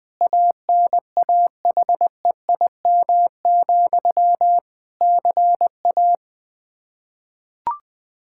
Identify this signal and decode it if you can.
Morse code, heard as a keyed tone, about 21 wpm, 700 Hz. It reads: ANAHEIM, CA
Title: Practice QSO - Exchange 1 - Beginning - 20wpm